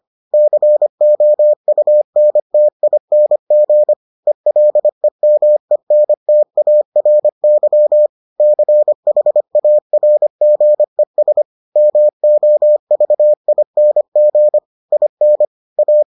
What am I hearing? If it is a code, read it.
Morse code, heard as a keyed tone, about 25 wpm, 600 Hz. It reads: COUNTING ELEMENTARY CHARGES MOVING IN A